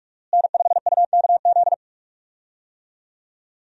Morse code, heard as a keyed tone, about 41 wpm, 700 Hz. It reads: N5UKZ
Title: USA Call Signs - 5 Characters - Repeated 2x - 45wpm